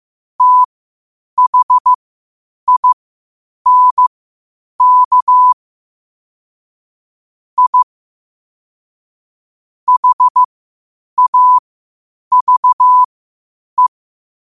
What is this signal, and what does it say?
Morse code, heard as a keyed tone, about 15 wpm, spaced as for 8 wpm, 1000 Hz. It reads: THINK I HAVE
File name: qotdfiles / 8wpm